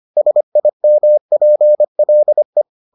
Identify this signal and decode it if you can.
Morse code, heard as a keyed tone, about 25 wpm, 600 Hz. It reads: SIMPLE